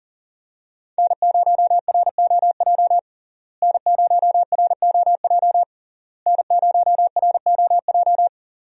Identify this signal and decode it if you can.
Morse code, heard as a keyed tone, about 40 wpm, 700 Hz. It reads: N0ROJ N0ROJ N0ROJ
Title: USA Call Signs - 5 Characters - Repeated 3x - 40wpm